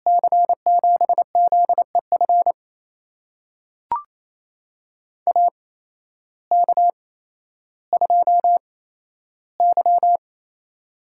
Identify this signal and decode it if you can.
Morse code, heard as a keyed tone, about 28 wpm, spaced as for 13 wpm, 700 Hz. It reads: C7ZEF AK2Y
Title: USA Call Signs - 8x Character Spacing - 28wpm